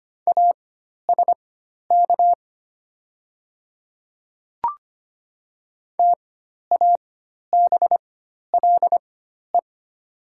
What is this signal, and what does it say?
Morse code, heard as a keyed tone, about 25 wpm, spaced as for 12 wpm, 690 Hz. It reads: ASK TABLE
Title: 4x Character Spacing - Top 500 Words - 25wpm